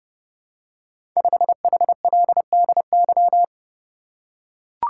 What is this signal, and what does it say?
Morse code, heard as a keyed tone, about 30 wpm, 700 Hz. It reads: 5HLDY
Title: Code Groups - 5 Letter:Numbers - 30wpm